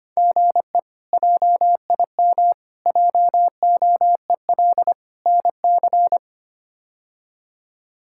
Morse code, heard as a keyed tone, about 25 wpm, 700 Hz. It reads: GE JIM JOEL NC
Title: K1USN SST QSO Practice - 25wpm